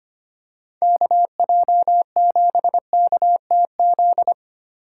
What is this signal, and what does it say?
Morse code, heard as a keyed tone, about 25 wpm, 700 Hz. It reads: KJ7KTZ
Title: USA Call Signs - 6 Characters - 4x Character Spacing - 25wpm